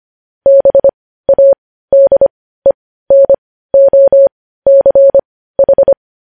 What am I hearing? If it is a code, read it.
Morse code, heard as a keyed tone, about 25 wpm, spaced as for 15 wpm, 550 Hz. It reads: BADENOCH
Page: news headlines in morse code at 15 wpm